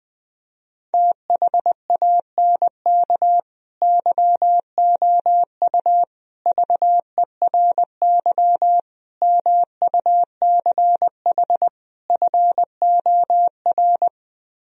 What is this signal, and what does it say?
Morse code, heard as a keyed tone, about 20 wpm, 700 Hz. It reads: THANK YOU VERY MUCH FOR